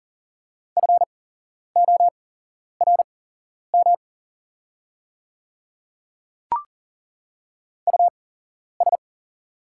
Morse code, heard as a keyed tone, about 40 wpm, spaced as for 11 wpm, 690 Hz. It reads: FORM US
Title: 8x Character Spacing - Top 500 Words - 40wpm